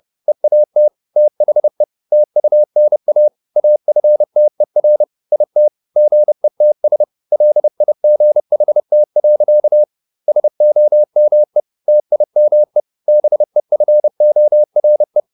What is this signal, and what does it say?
Morse code, heard as a keyed tone, about 30 wpm, 600 Hz. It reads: EAT THE TUNA AFTER IT GETS LIGHT. SOME TIME BEFORE
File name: the old man and the sea / the old man and the sea 1 30wpm